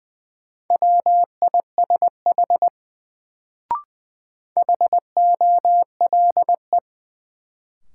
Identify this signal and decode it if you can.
Morse code, heard as a keyed tone, about 20 wpm, 700 Hz. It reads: WISH HOLE
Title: Sentences from Top 1600 Words - New Word Drill - 20wpm